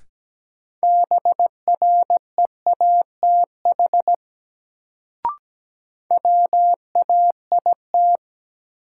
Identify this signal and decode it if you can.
Morse code, heard as a keyed tone, about 17 wpm, 700 Hz. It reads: BREATH WAIT
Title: Sets of 5 Words Encore - New Word Drill - 17wpm